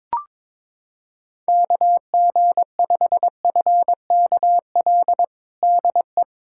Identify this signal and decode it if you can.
Morse code, heard as a keyed tone, about 22 wpm, 700 Hz. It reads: KG5FKL DE
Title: USA Callsign DE Callsign 22wpm